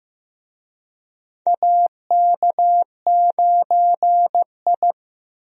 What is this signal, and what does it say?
Morse code, heard as a keyed tone, about 15 wpm, 700 Hz. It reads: AK9I
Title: USA Call Signs - 4 Characters - Repeated 3x - 15wpm